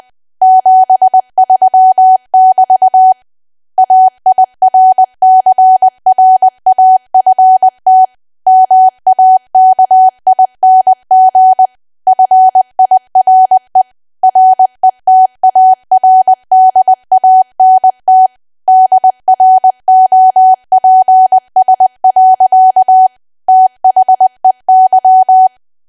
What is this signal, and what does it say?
Morse code, heard as a keyed tone, about 20 wpm, 750 Hz. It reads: 73<BT> AIRCRAFT MAKING FIRE RETARDANT DROPS. THEY
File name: Archive / 20 WPM / 20WPM